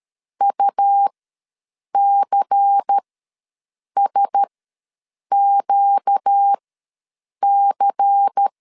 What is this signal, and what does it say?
Morse code, heard as a keyed tone, about 13 wpm, spaced as for 7 wpm, 800 Hz. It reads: UCSQC
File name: LEZIONI / 5 LBVC / 5 LBVC ws 12wpm 3159